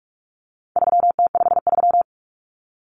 Morse code, heard as a keyed tone, about 44 wpm, 700 Hz. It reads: 3T53
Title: Common Cut Numbers - 4-Digit 45wpm